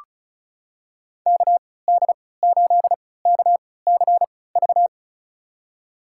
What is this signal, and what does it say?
Morse code, heard as a keyed tone, about 36 wpm, spaced as for 20 wpm, 700 Hz. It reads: KD8KCV
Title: USA Call Signs - 6 Characters - 3x Character Spacing - 35wpm